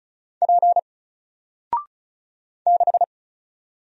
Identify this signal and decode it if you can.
Morse code, heard as a keyed tone, about 36 wpm, 700 Hz. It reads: P 6
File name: Single-Letter-Number-35wpm